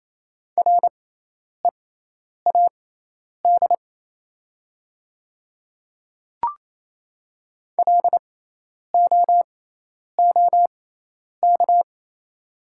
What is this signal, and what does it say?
Morse code, heard as a keyed tone, about 28 wpm, spaced as for 10 wpm, 700 Hz. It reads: READ LOOK
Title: 6x Character Spacing - Top 200 Words - 28wpm